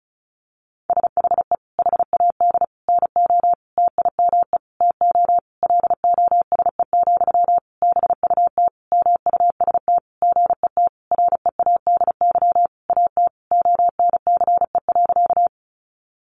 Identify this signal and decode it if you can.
Morse code, heard as a keyed tone, about 34 wpm, 700 Hz. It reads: SHE HAD NO TIME TO LOSE, BUT MUST GET READY AT ONCE.
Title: Sentences from Top 1100 Words 35wpm